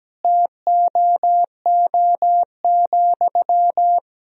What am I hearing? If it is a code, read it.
Morse code, heard as a keyed tone, about 17 wpm, 700 Hz. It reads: TOO,
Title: Sentences from Top 1000 Words - Review Part 3 - 17wpm